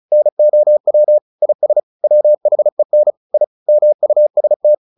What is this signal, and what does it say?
Morse code, heard as a keyed tone, about 34 wpm, 600 Hz. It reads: NOW IS WHEN I MUST
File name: the old man and the sea / the old man and the sea 1 35wpm